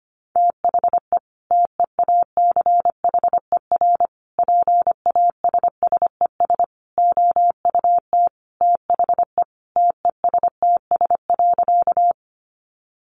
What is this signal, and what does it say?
Morse code, heard as a keyed tone, about 25 wpm, 700 Hz. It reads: THE TEACHER PASSES OUT THE TESTS.
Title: Additional Easy Phrases 25wpm